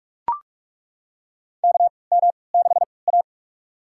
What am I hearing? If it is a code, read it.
Morse code, heard as a keyed tone, about 42 wpm, spaced as for 25 wpm, 700 Hz. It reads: KM6A
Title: USA Call Signs - 4 Characters - 3x Character Spacing - 45wpm